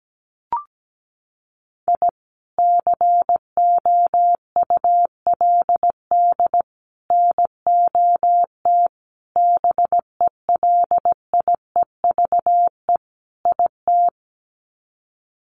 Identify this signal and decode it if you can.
Morse code, heard as a keyed tone, about 17 wpm, 700 Hz. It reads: I COULD NOT BELIEVE IT
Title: Sets of 5 Words Encore 17wpm